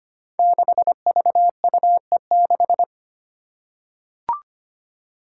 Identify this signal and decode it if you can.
Morse code, heard as a keyed tone, about 25 wpm, 700 Hz. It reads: 6VUE6
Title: Five Challenging Letter-Number Combinations 25wpm